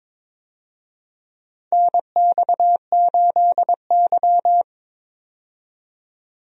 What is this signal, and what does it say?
Morse code, heard as a keyed tone, about 22 wpm, 700 Hz. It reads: NX8Y
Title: USA Call Signs - 4 Characters - 22wpm